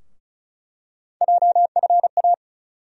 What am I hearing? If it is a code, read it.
Morse code, heard as a keyed tone, about 34 wpm, 700 Hz. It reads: JFA
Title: USA Call Sign Postfixes - 3 Letters 35wpm